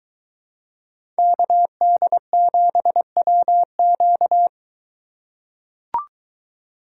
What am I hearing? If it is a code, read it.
Morse code, heard as a keyed tone, about 22 wpm, 700 Hz. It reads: KD7WQ